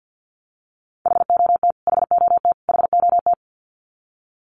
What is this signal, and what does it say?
Morse code, heard as a keyed tone, about 50 wpm, 700 Hz. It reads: HOT HOT HOT
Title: Top 500 Words Repeated 50wpm